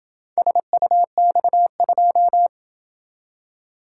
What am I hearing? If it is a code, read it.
Morse code, heard as a keyed tone, about 26 wpm, 700 Hz. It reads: SUX2